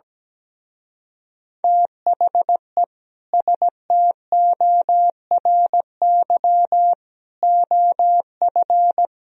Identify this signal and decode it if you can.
Morse code, heard as a keyed tone, about 19 wpm, 700 Hz. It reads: THE STORY OF
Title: Speed-Racing - Sets of 3 Words 17wpm